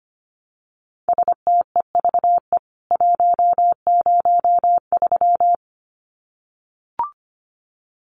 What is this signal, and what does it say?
Morse code, heard as a keyed tone, about 25 wpm, 700 Hz. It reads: STEVE 103
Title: CWT Exchange 25wpm